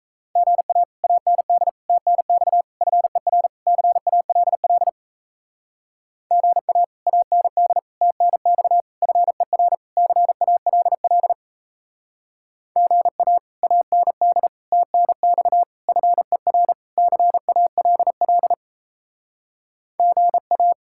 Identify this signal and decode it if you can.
Morse code, heard as a keyed tone, about 36 wpm, 700 Hz. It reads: GA AND TNX FER CALL GA AND TNX FER CALL GA AND TNX FER CALL GA